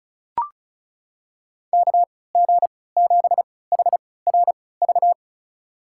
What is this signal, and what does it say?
Morse code, heard as a keyed tone, about 36 wpm, spaced as for 20 wpm, 700 Hz. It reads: KG7HRV
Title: USA Call Signs - 6 Characters - 3x Character Spacing - 35wpm